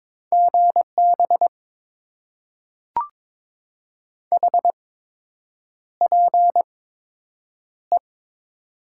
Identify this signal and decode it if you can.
Morse code, heard as a keyed tone, about 22 wpm, spaced as for 8 wpm, 700 Hz. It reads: GB HPE
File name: Top 90 QSO Elements - 8x Character Spacing - 22wpm